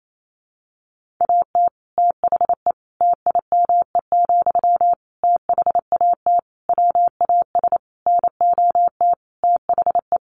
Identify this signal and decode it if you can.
Morse code, heard as a keyed tone, about 28 wpm, 700 Hz. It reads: AT THE TIME, THAT WAS NOT THE
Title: Sentences from Top 1400 Words 28wpm